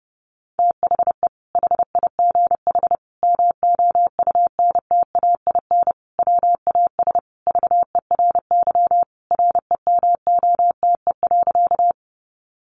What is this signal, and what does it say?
Morse code, heard as a keyed tone, about 30 wpm, 700 Hz. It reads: THE HIGH MOUNTAIN WAS VERY REMOTE.
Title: Sentences from Top 1300 Words 30wpm